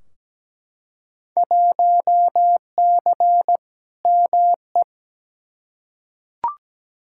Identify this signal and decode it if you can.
Morse code, heard as a keyed tone, about 17 wpm, 700 Hz. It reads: 1C ME
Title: Field Day Exchanges - 17wpm